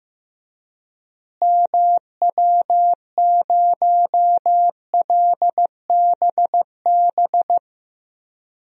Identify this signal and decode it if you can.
Morse code, heard as a keyed tone, about 15 wpm, 700 Hz. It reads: MW0LBB
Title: DXCC-100 Call Signs - 6 Characters - Repeated 2x - 15wpm